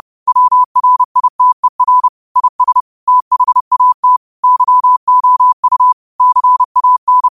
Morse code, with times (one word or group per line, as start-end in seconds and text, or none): 0.3-2.1 WRITER
2.4-2.8 IS
3.1-4.2 THAT
4.4-5.9 YOU
6.2-7.3 CAN